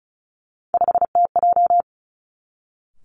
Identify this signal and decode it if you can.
Morse code, heard as a keyed tone, about 36 wpm, 700 Hz. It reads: 5TJ
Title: Three Challenging Letter-Number Combinations 35wpm